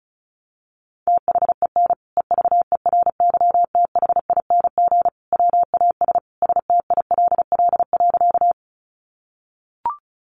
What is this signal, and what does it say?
Morse code, heard as a keyed tone, about 34 wpm, 700 Hz. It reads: THEN EVERYTHING WAS STILL.
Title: Sentences from Top 700 Words 35wpm